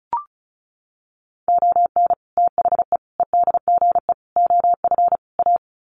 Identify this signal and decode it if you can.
Morse code, heard as a keyed tone, about 34 wpm, 700 Hz. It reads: ON THE EDGE OF A
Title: Sets of 5 Words Encore2 35wpm